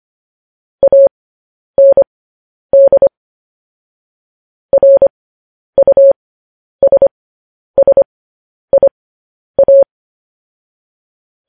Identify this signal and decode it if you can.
Morse code, heard as a keyed tone, about 25 wpm, spaced as for 10 wpm, 550 Hz. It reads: AND RUSSIA